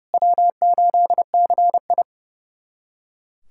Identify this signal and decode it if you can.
Morse code, heard as a keyed tone, about 32 wpm, 700 Hz. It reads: W8CI
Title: USA Callsign DE Callsign - 4 Characters - 30wpm